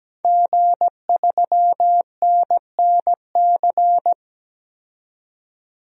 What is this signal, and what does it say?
Morse code, heard as a keyed tone, about 17 wpm, 700 Hz. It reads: G3NNC
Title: DXCC-100 Call Signs - 5 Characters - 17wpm